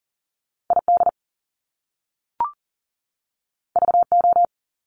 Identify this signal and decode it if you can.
Morse code, heard as a keyed tone, about 41 wpm, 700 Hz. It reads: ID VO
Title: Top Two Letter Combinations in Words 40wpm